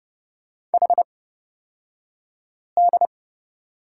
Morse code, heard as a keyed tone, about 29 wpm, spaced as for 5 wpm, 700 Hz. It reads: HD
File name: Lesson 36d - BT - Two Characters 30wpm